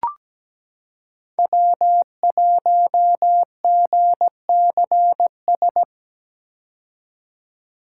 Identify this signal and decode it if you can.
Morse code, heard as a keyed tone, about 17 wpm, 700 Hz. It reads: W1GCS
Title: K1USN SST QSO Practice - 17wpm